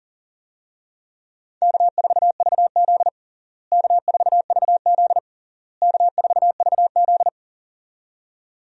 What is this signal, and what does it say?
Morse code, heard as a keyed tone, about 40 wpm, 700 Hz. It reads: K4VZ K4VZ K4VZ
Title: USA Call Signs - 4 Characters - Repeated 3x - 40wpm